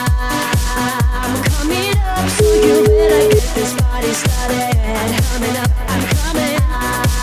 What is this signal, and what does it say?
Ringback tone (Europe, 425 Hz), with noise.